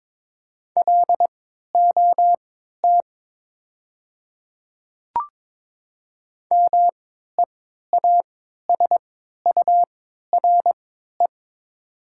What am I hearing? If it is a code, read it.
Morse code, heard as a keyed tone, about 22 wpm, spaced as for 12 wpm, 700 Hz. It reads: LOT MEASURE
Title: 3x Character Spacing - Top 500 Words - 22wpm